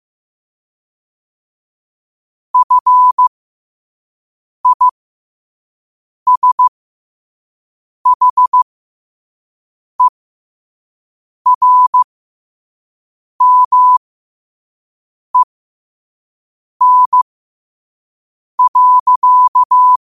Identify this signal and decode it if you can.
Morse code, heard as a keyed tone, about 15 wpm, spaced as for 5 wpm, 1000 Hz. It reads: FISHERMEN.
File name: qotdfiles / 5wpm